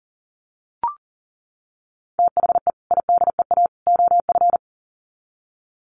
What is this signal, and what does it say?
Morse code, heard as a keyed tone, about 40 wpm, 700 Hz. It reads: THE IDEA OF
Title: Sets of 3 Words 40wpm